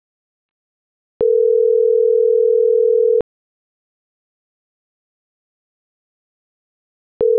Ringback tone (North America), 440 + 480 Hz.